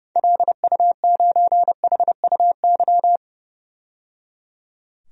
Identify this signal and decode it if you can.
Morse code, heard as a keyed tone, about 30 wpm, 700 Hz. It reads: LU9HUY
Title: DXCC-100 Call Signs - 6 Characters - 30wpm